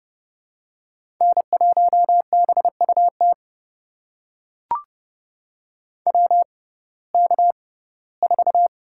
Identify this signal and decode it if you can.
Morse code, heard as a keyed tone, about 30 wpm, spaced as for 18 wpm, 700 Hz. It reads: N1BUT WK4
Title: USA Call Signs - 6x Character Spacing - 30wpm